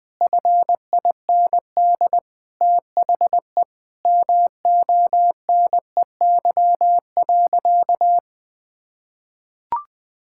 Morse code, heard as a keyed tone, about 20 wpm, 700 Hz. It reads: FIND THE MONEY.
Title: Sentences from Top 1600 Words 20wpm